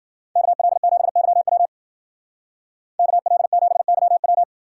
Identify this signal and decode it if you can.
Morse code, heard as a keyed tone, about 51 wpm, 700 Hz. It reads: KL7YW KL7YW